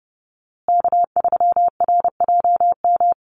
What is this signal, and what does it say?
Morse code, heard as a keyed tone, about 30 wpm, 700 Hz. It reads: K3RJM
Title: USA Call Signs Repeated 2x 30wpm